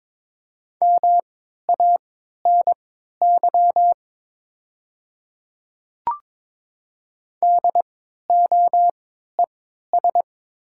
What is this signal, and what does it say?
Morse code, heard as a keyed tone, about 22 wpm, spaced as for 13 wpm, 700 Hz. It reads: MANY DOES
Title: 3x Character Spacing - Top 200 Words - 22wpm